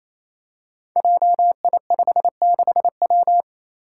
Morse code, heard as a keyed tone, about 28 wpm, 700 Hz. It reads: JI56W
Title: Five Challenging Letter-Number Combinations 28wpm